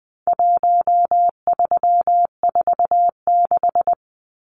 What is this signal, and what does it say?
Morse code, heard as a keyed tone, about 20 wpm, 700 Hz. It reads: 1346
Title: Numbers - Contest Style - 4-Digit 20wpm